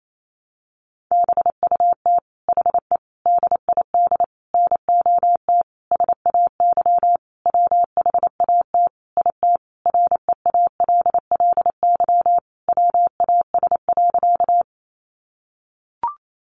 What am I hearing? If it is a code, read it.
Morse code, heard as a keyed tone, about 28 wpm, 700 Hz. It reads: BUT HE DID NOT SAY WHAT IT REALLY WAS.
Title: Sentences from Top 600 Words 28wpm